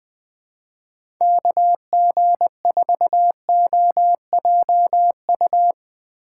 Morse code, heard as a keyed tone, about 20 wpm, 700 Hz. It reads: KG4OJU
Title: USA Call Signs - 6 Characters - 2x Character Spacing - 20wpm